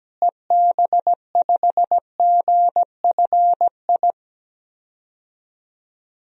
Morse code, heard as a keyed tone, about 17 wpm, 700 Hz. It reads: EB5GFI
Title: DXCC-100 Call Signs - 6 Characters - Repeated 2x - 17wpm